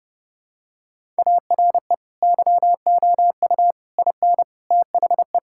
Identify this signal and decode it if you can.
Morse code, heard as a keyed tone, about 32 wpm, 700 Hz. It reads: ARE YOU IN THE
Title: Sentences from Top 1000 Words - Review Part 5 - 30wpm